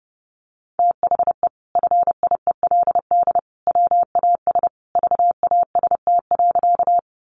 Morse code, heard as a keyed tone, about 30 wpm, 700 Hz. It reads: THE FIELD WAS VAST.
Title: Sentences from Top 1300 Words 30wpm